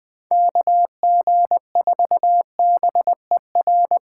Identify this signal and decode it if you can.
Morse code, heard as a keyed tone, about 23 wpm, 700 Hz. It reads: KG4BER
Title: Speed Racing - USA Call Signs 20wpm